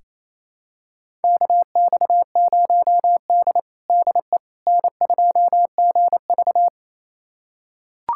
Morse code, heard as a keyed tone, about 27 wpm, 690 Hz. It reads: KX0D DE N2GV
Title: USA Callsign DE Callsign - 4 Characters - 28wpm